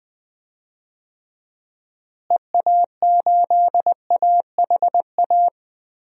Morse code, heard as a keyed tone, about 20 wpm, 700 Hz. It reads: EA8AHA
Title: DXCC-100 Call Signs - 6 Characters - Repeated 2x - 20wpm